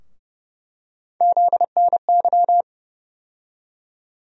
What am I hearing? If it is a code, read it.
Morse code, heard as a keyed tone, about 31 wpm, 700 Hz. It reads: ZNY
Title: USA Call Sign Postfixes - 3 Letters 30wpm